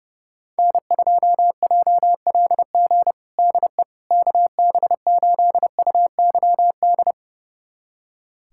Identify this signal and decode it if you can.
Morse code, heard as a keyed tone, about 32 wpm, 700 Hz. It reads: N2JLG DE KB8UYD
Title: USA Callsign DE Callsign 30wpm